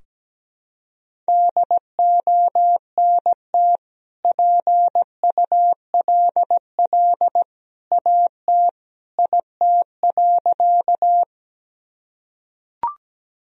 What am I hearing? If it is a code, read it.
Morse code, heard as a keyed tone, about 17 wpm, 700 Hz. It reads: DONT PULL AT IT.
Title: Sentences from Top 1000 Words - Review Part 2 - 17wpm